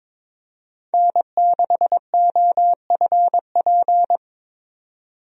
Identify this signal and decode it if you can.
Morse code, heard as a keyed tone, about 21 wpm, 700 Hz. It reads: N6OFP
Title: USA Call Signs - 5 Characters - 22wpm